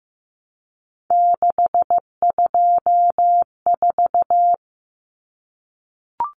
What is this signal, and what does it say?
Morse code, heard as a keyed tone, about 15 wpm, 700 Hz. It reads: 624
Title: Numbers - 3-Digit 15wpm